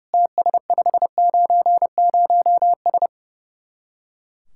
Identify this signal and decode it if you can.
Morse code, heard as a keyed tone, about 36 wpm, 700 Hz. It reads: TS590S
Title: Speed-Racing Common QSO Phrases - 30wpm